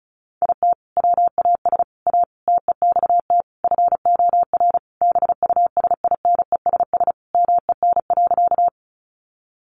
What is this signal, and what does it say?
Morse code, heard as a keyed tone, about 34 wpm, 700 Hz. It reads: IT WAS A TEXT FOR BUSINESS MEN.
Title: Sentences from Top 1200 Words 35wpm